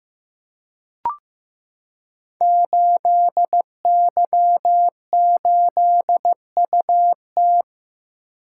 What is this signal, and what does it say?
Morse code, heard as a keyed tone, about 15 wpm, 700 Hz. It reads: 8Y8UT
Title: Code Groups - 5 Letter:Numbers - 15wpm